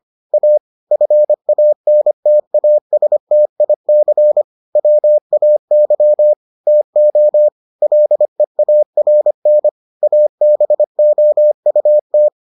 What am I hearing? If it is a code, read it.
Morse code, heard as a keyed tone, about 25 wpm, 600 Hz. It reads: A FANTASTIC WAY TO LEARN ABOUT